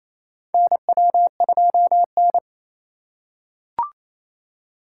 Morse code, heard as a keyed tone, about 28 wpm, 700 Hz. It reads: NW2N
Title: USA Call Signs - 4 Characters - 2x Character Spacing - 28wpm